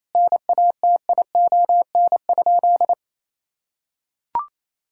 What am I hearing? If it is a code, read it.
Morse code, heard as a keyed tone, about 28 wpm, 700 Hz. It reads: NATION?
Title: Sentences from Top 1000 Words - Review Part 6 - 28wpm